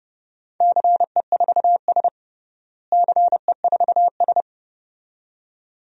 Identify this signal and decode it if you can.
Morse code, heard as a keyed tone, about 31 wpm, 700 Hz. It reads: CE4S CE4S
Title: DXCC-100 Call Signs - 4 Characters - Repeated 2x - 30wpm